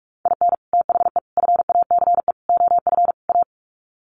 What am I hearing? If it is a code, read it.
Morse code, heard as a keyed tone, about 44 wpm, 700 Hz. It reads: IN THE FACE OF A
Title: Sets of 5 Words Encore 45wpm